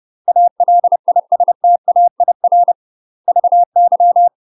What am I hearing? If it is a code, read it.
Morse code, heard as a keyed tone, about 30 wpm, 700 Hz. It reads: ALISTAIR VY
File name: UK MASTER QSOs TRAINING / 30WPM / UK MASTER QSOs 0045